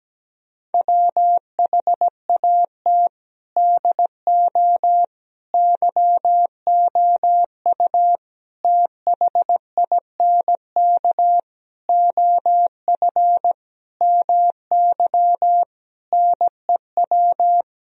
Morse code, heard as a keyed tone, about 17 wpm, 700 Hz. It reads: WHAT DO YOU THINK OF MY NEW
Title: Sentences from Top 1000 Words - Review Part 4 - 17wpm